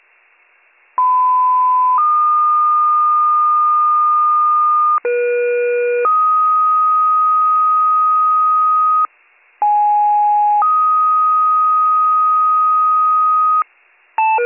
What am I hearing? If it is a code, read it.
Two-tone page, heard as a one-second tone, then a three-second tone, about 1010 then 1220 Hz, about 480 then 1250 Hz, about 830 then 1250 Hz. It is heard from 1.0 to 13.6 s.